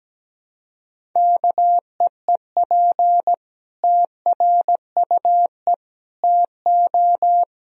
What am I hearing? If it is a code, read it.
Morse code, heard as a keyed tone, about 17 wpm, 700 Hz. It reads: KEEP TRUE TO
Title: Sentences from Top 1000 Words - Review Part 3 - 17wpm